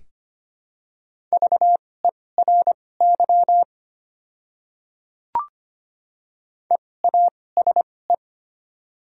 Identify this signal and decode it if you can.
Morse code, heard as a keyed tone, about 25 wpm, spaced as for 18 wpm, 700 Hz. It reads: VERY EASE
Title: 2x Character Spacing - Top 500 Words - 25wpm